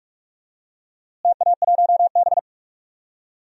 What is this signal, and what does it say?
Morse code, heard as a keyed tone, about 42 wpm, 700 Hz. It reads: TA1B
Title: DXCC-100 Call Signs - 4 Characters - Repeated 3x - 45wpm